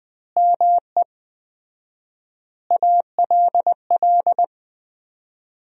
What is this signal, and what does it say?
Morse code, heard as a keyed tone, about 20 wpm, 700 Hz. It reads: ME ALL
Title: 4x Word-Spacing - Sentences from Top 200 Words - 20wpm